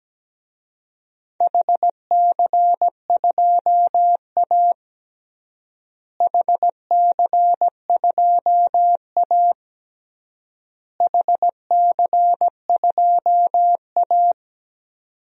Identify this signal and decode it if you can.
Morse code, heard as a keyed tone, about 17 wpm, 700 Hz. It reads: HC2A HC2A HC2A
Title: DXCC-100 Call Signs - 4 Characters - Repeated 3x - 17wpm